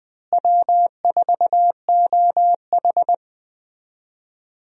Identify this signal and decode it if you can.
Morse code, heard as a keyed tone, about 20 wpm, 700 Hz. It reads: W4OH
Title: USA Call Signs - 4 Characters - Repeated 3x - 20wpm